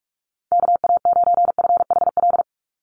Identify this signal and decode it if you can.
Morse code, heard as a keyed tone, about 44 wpm, 700 Hz. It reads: KA9FHL
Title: USA Call Signs 45wpm